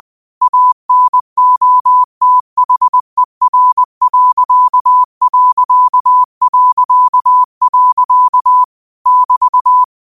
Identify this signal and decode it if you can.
Morse code, heard as a keyed tone, about 20 wpm, 1000 Hz. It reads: ANOTHER.... <BT>